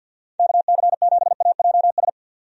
Morse code, heard as a keyed tone, about 51 wpm, 700 Hz. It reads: KC7AJS
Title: USA Call Signs - 6 Characters - 50wpm